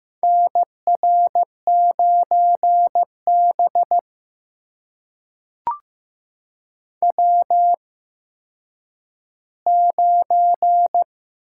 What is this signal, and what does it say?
Morse code, heard as a keyed tone, about 15 wpm, spaced as for 9 wpm, 700 Hz. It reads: NR9B W9
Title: USA Call Signs - 4 Characters - 8x Character Spacing - 15wpm